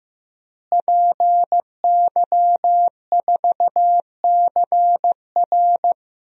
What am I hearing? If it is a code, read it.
Morse code, heard as a keyed tone, about 15 wpm, 700 Hz. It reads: PY4CR
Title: DXCC-100 Call Signs - 5 Characters - Repeated 2x - 15wpm